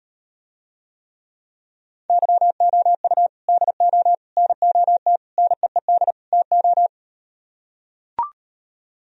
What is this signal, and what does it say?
Morse code, heard as a keyed tone, about 39 wpm, 700 Hz. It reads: YOU DO NOT NEED TO